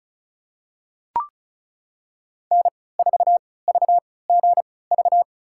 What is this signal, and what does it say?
Morse code, heard as a keyed tone, about 36 wpm, spaced as for 20 wpm, 700 Hz. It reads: N4VGV
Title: USA Call Signs - 5 Characters - 3x Character Spacing - 35wpm